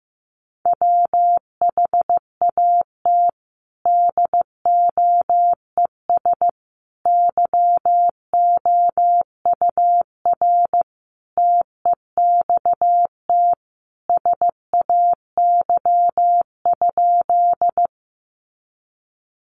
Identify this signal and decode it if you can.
Morse code, heard as a keyed tone, about 15 wpm, 700 Hz. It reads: WHAT DOES YOUR TEXT SAY?
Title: Sentences from Top 1200 Words 15wpm